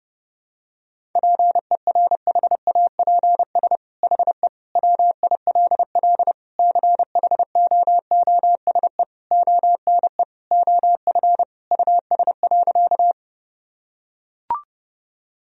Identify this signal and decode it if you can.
Morse code, heard as a keyed tone, about 30 wpm, 700 Hz. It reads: PERHAPS HE WILL CHOOSE ONE OF US.
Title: Sentences from Top 1000 Words - Review Part 4 - 30wpm